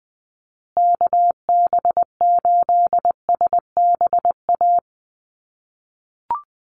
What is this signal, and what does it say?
Morse code, heard as a keyed tone, about 20 wpm, 700 Hz. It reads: KB8SBA
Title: USA Call Signs Repeated 3x 20wpm